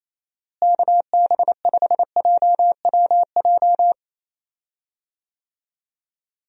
Morse code, heard as a keyed tone, about 28 wpm, 700 Hz. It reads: KB5JWJ
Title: USA Call Signs - 6 Characters - Repeated 2x - 28wpm